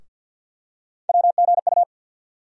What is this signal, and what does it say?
Morse code, heard as a keyed tone, about 53 wpm, 700 Hz. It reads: WGU
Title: USA Call Sign Postfixes - 3 Letters 50wpm